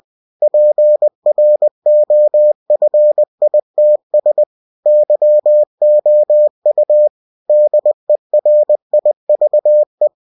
Morse code, heard as a keyed tone, about 20 wpm, 600 Hz. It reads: PROFITS YOU DERIVE
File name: cw-books / winnie-the-pooh / winnie-20wpm-page-0117